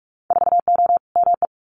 Morse code, heard as a keyed tone, about 46 wpm, 690 Hz. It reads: 4O ME
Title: Winter Field Day Exchanges Encore 45wpm